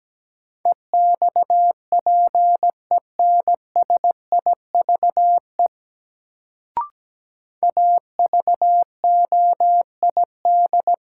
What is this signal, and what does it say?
Morse code, heard as a keyed tone, about 17 wpm, 700 Hz. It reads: EXPENSIVE AVOID
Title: Sentences from Top 1000 Words - Review Part 6 - New Word Drill - 17wpm